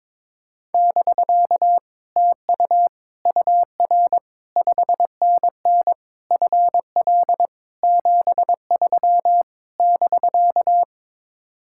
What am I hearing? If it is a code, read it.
Morse code, heard as a keyed tone, about 22 wpm, 700 Hz. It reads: <BK> TU UR 5NN FL 73 <BK>
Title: Speed-Racing Common QSO Phrases Encore - 22wpm